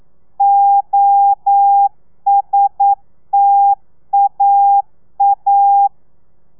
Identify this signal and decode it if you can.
Morse code, heard as a keyed tone, about 9 wpm, 800 Hz. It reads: OSTAA
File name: telegrafia / i2tfj / Lez2 40